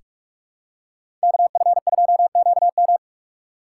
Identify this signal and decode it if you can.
Morse code, heard as a keyed tone, about 46 wpm, 700 Hz. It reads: KF2QM
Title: USA Call Signs - 5 Characters - 6x Character Spacing - 45wpm